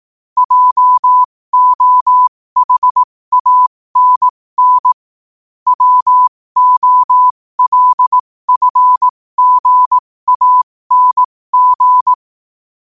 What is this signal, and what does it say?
Morse code, heard as a keyed tone, about 18 wpm, spaced as for 15 wpm, 1000 Hz. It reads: JOHANN WOLFGANG
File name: qotdfiles / 15wpm